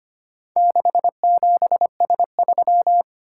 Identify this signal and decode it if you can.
Morse code, heard as a keyed tone, about 25 wpm, 700 Hz. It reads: 67S3
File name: practice / 25wpm / Four Letter-Number Combinations - 25wpm